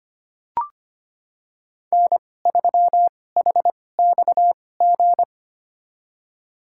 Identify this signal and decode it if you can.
Morse code, heard as a keyed tone, about 25 wpm, spaced as for 18 wpm, 700 Hz. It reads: N3HXG
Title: USA Call Signs - 5 Characters - 2x Character Spacing - 25wpm